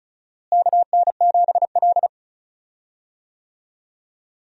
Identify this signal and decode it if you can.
Morse code, heard as a keyed tone, about 36 wpm, 700 Hz. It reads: KN7L
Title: USA Call Signs - 4 Characters - Repeated 3x - 35wpm